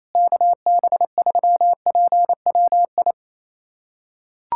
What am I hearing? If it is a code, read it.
Morse code, heard as a keyed tone, about 28 wpm, 700 Hz. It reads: KB3PWI
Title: USA Callsign DE Callsign 28wpm